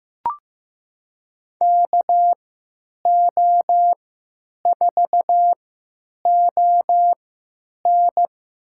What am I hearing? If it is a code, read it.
Morse code, heard as a keyed tone, about 15 wpm, spaced as for 9 wpm, 700 Hz. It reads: KO4ON
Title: USA Call Signs - 5 Characters - 3x Character Spacing - 15wpm